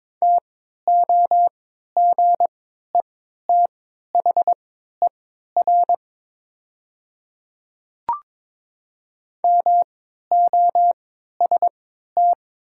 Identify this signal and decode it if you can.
Morse code, heard as a keyed tone, about 22 wpm, spaced as for 13 wpm, 700 Hz. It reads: TOGETHER MOST
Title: 3x Character Spacing - Top 300 Words - 22wpm